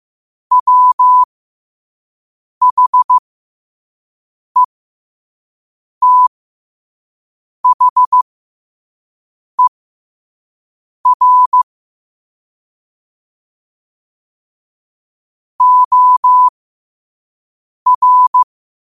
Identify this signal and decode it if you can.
Morse code, heard as a keyed tone, about 15 wpm, spaced as for 5 wpm, 1000 Hz. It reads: WHETHER OR
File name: qotdfiles / 5wpm